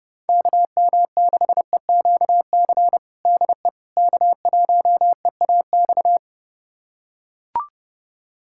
Morse code, heard as a keyed tone, about 30 wpm, 700 Hz. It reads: KM6EQC DE K1EAX
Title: USA Callsign DE Callsign 30wpm